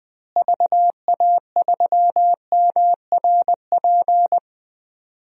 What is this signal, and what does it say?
Morse code, heard as a keyed tone, about 20 wpm, 700 Hz. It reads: VA3MRP
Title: DXCC-100 Call Signs - 6 Characters - Repeated 2x - 20wpm